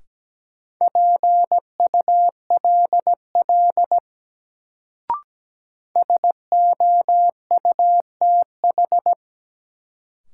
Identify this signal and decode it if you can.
Morse code, heard as a keyed tone, about 17 wpm, 700 Hz. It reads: PULL SOUTH
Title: Sentences from Top 400 Words - New Word Drill - 17wpm